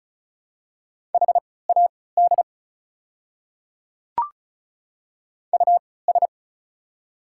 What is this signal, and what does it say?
Morse code, heard as a keyed tone, about 36 wpm, spaced as for 20 wpm, 690 Hz. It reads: HAD US
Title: 3x Character Spacing - Top 200 Words - 35wpm